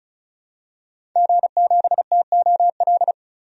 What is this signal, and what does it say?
Morse code, heard as a keyed tone, about 36 wpm, 700 Hz. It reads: G7TOL
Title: DXCC-100 Call Signs - 5 Characters - 35wpm